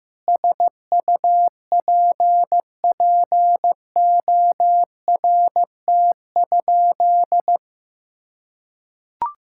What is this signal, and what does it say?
Morse code, heard as a keyed tone, about 15 wpm, 700 Hz. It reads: SUPPORT?
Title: Sentences from Top 1000 Words - Review Part 5 - 15wpm